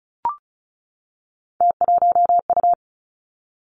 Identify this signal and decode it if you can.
Morse code, heard as a keyed tone, about 36 wpm, 700 Hz. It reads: T1U
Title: Three Challenging Letter-Number Combinations 35wpm